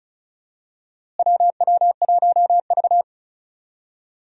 Morse code, heard as a keyed tone, about 36 wpm, 700 Hz. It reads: WW1V
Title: USA Call Signs - 4 Characters - Repeated 2x - 35wpm